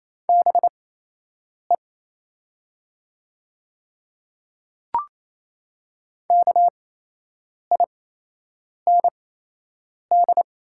Morse code, heard as a keyed tone, about 28 wpm, spaced as for 8 wpm, 700 Hz. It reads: BE KIND
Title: 8x Character Spacing - Top 200 Words - 28wpm